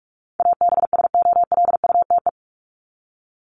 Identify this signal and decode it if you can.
Morse code, heard as a keyed tone, about 44 wpm, 700 Hz. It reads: ABSOLUTE
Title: LY Suffix - Top 100 Words with Base Words 45wpm